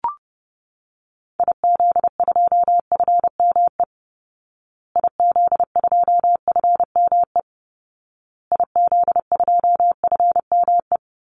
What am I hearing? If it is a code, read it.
Morse code, heard as a keyed tone, about 30 wpm, 700 Hz. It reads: IZ2FME IZ2FME IZ2FME
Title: Call Signs Repeated 2x - 30wpm